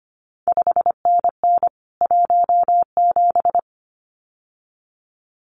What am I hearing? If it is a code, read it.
Morse code, heard as a keyed tone, about 25 wpm, 700 Hz. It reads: 5NN 17
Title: CQ WW DX Contest Exchange Encore - 25wpm